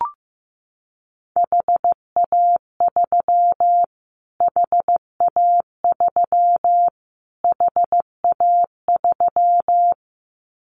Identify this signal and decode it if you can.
Morse code, heard as a keyed tone, about 15 wpm, 700 Hz. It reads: HA3 HA3 HA3
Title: Callsign Prefixes for Top 25 DXCC - Repeated 3x - 15wpm